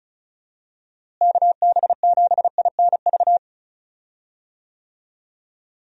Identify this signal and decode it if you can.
Morse code, heard as a keyed tone, about 36 wpm, 700 Hz. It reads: KB7INV
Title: USA Call Signs - 6 Characters - Repeated 2x - 35wpm